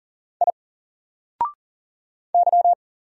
Morse code, heard as a keyed tone, about 40 wpm, 700 Hz. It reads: I Y
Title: Single Character (Letter:Number:Punctuation:Prosign) Alt 40wpm